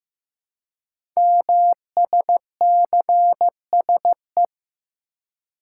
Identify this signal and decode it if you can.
Morse code, heard as a keyed tone, about 15 wpm, 700 Hz. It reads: MSCSE
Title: Code Groups - 5 Letter:Numbers - 15wpm